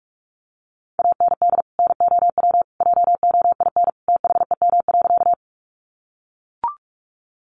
Morse code, heard as a keyed tone, about 44 wpm, 700 Hz. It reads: AND NOW JOIN THEM.
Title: Sentences from Top 1400 Words 45wpm